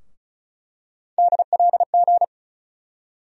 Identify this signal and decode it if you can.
Morse code, heard as a keyed tone, about 34 wpm, 700 Hz. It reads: DLG
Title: USA Call Sign Postfixes - 3 Letters 35wpm